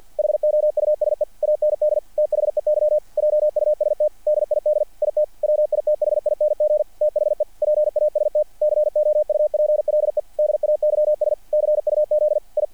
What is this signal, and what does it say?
Morse code, heard as a keyed tone, about 48 wpm, 660 Hz. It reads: HOUSE AND THEY JUST DID IT WITHING THE PAST COUPLE DAYS CUZ I
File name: archived / CRSnet